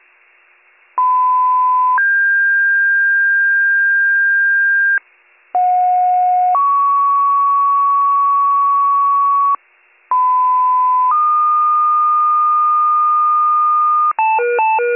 Two-tone page, heard as a one-second tone, then a three-second tone, about 1010 then 1600 Hz, about 730 then 1120 Hz, about 1010 then 1220 Hz. It is heard from 1.0 to 14.1 s.